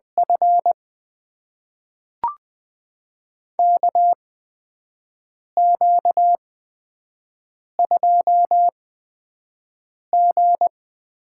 Morse code, heard as a keyed tone, about 20 wpm, spaced as for 5 wpm, 700 Hz. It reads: F KQ2G
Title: USA Call Signs - 8x Character Spacing - 20wpm